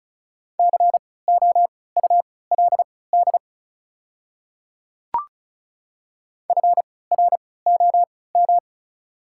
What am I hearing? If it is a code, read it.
Morse code, heard as a keyed tone, about 36 wpm, spaced as for 20 wpm, 690 Hz. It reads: COULD FROM
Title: 3x Character Spacing - Top 100 Words - 35wpm